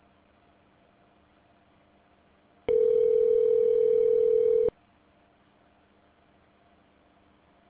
Ringback tone (North America), 440 + 480 Hz.